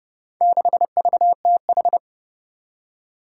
Morse code, heard as a keyed tone, about 30 wpm, 700 Hz. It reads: 6VTH
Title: Four Challenging Letter-Number Combinations 30wpm